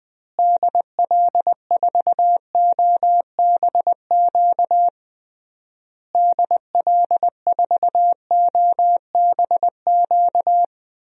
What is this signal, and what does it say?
Morse code, heard as a keyed tone, about 20 wpm, 700 Hz. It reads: DL4OBQ DL4OBQ
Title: USA Call Signs - 6 Characters - Repeated 3x - 20wpm